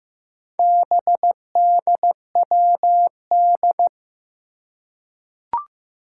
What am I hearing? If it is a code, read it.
Morse code, heard as a keyed tone, about 15 wpm, 700 Hz. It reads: BDWD
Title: Four Challenging Letter-Number Combinations 15wpm